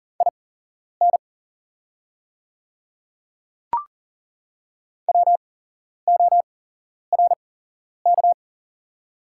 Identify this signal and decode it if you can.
Morse code, heard as a keyed tone, about 40 wpm, spaced as for 11 wpm, 690 Hz. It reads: IN WORK
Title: 8x Character Spacing - Top 300 Words - 40wpm